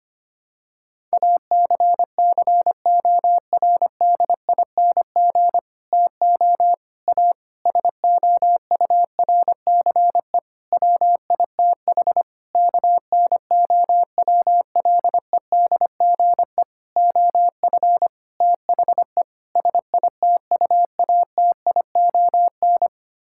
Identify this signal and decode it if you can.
Morse code, heard as a keyed tone, about 25 wpm, 700 Hz. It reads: ACCORDING TO A SOURCE WITH KNOWLEDGE OF THE SITUATION
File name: Sets of 9 Words - 25wpm